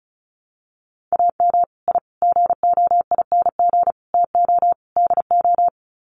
Morse code, heard as a keyed tone, about 34 wpm, 700 Hz. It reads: AM I GOING TO DO
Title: Sets of 5 Words Encore 35wpm